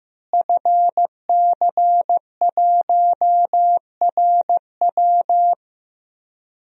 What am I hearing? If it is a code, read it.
Morse code, heard as a keyed tone, about 15 wpm, 700 Hz. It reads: FC1RW
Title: DXCC-100 Call Signs - 5 Characters - Repeated 3x - 15wpm